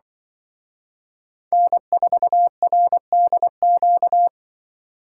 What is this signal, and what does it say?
Morse code, heard as a keyed tone, about 24 wpm, 700 Hz. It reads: N4RDQ
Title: Speed Racing - USA Call Signs 20wpm